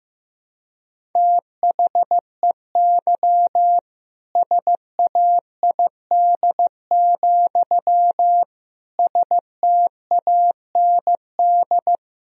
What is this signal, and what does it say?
Morse code, heard as a keyed tone, about 15 wpm, 700 Hz. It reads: THEY SAID, STAND
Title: Sentences from Top 1000 Words - Review Part 1 - 15wpm